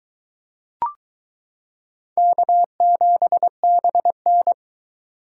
Morse code, heard as a keyed tone, about 22 wpm, 700 Hz. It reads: K7BN